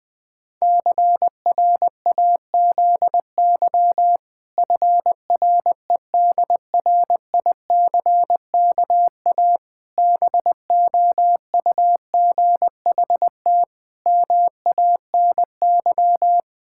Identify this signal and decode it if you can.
Morse code, heard as a keyed tone, about 21 wpm, 700 Hz. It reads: CRAZY FREDRICKA BOUGHT MANY
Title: Pangram Challenge v2 - 20wpm